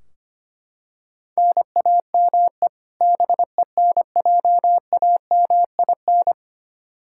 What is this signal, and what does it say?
Morse code, heard as a keyed tone, about 25 wpm, 700 Hz. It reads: NAME BENJAMIN
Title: K1USN SST QSO Practice 25wpm